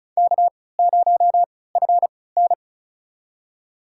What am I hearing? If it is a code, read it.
Morse code, heard as a keyed tone, about 36 wpm, spaced as for 20 wpm, 700 Hz. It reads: K0FN
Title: USA Call Signs - 3x Character Spacing - 35wpm